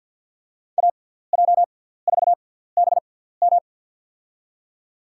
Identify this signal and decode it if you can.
Morse code, heard as a keyed tone, about 53 wpm, spaced as for 17 wpm, 690 Hz. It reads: AJ4BM
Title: USA Call Signs - 5 Characters - 6x Character Spacing - 50wpm